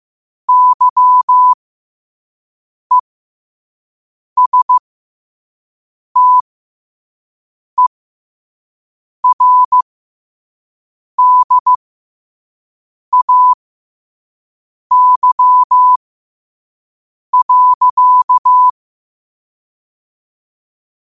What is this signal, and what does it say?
Morse code, heard as a keyed tone, about 15 wpm, spaced as for 5 wpm, 1000 Hz. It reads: YESTERDAY.
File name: qotdfiles / 5wpm